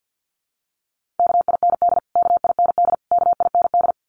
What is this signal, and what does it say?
Morse code, heard as a keyed tone, about 50 wpm, 700 Hz. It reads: KIND KIND KIND
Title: Top 200 Words Repeated 50wpm